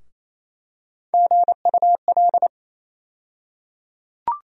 Morse code, heard as a keyed tone, about 28 wpm, 700 Hz. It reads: GUL
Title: USA Call Sign Postfixes - 3 Letters 28wpm